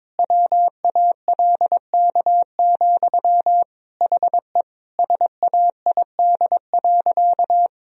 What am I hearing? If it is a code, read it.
Morse code, heard as a keyed tone, about 22 wpm, 700 Hz. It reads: WALK, HE SAID.
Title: Sentences from Top 1000 Words - Review Part 1 - 22wpm